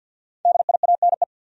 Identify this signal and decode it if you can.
Morse code, heard as a keyed tone, about 50 wpm, 700 Hz. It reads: DIANE
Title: USA Common Names - Max 5 Letters - 50wpm